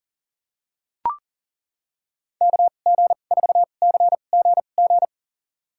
Morse code, heard as a keyed tone, about 40 wpm, spaced as for 29 wpm, 700 Hz. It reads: KG4CGG
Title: USA Call Signs - 6 Characters - 2x Character Spacing - 40wpm